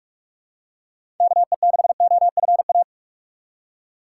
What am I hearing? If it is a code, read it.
Morse code, heard as a keyed tone, about 44 wpm, 700 Hz. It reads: KE6OFA
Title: USA Call Signs - 6 Characters - 6x Character Spacing - 45wpm